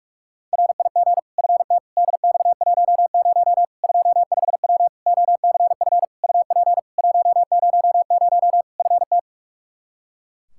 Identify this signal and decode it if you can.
Morse code, heard as a keyed tone, about 46 wpm, 700 Hz. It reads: RIG FT DX10 25W OCF UP 100 FT
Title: Practice QSO - Exchange 2 - Info - 45wpm